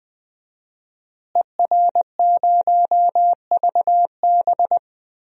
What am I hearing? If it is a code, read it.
Morse code, heard as a keyed tone, about 20 wpm, 700 Hz. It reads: ER0VB
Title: DXCC-100 Call Signs - 5 Characters - Repeated 3x - 20wpm